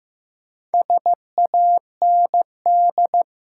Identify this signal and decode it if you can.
Morse code, heard as a keyed tone, about 15 wpm, 700 Hz. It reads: SAND
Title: Four Letter Words - 15wpm